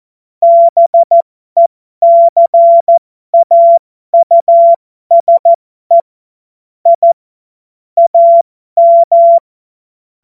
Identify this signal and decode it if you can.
Morse code, heard as a keyed tone, about 14 wpm, spaced as for 12 wpm, 680 Hz. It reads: BECAUSE I AM
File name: Morsekurs / Witze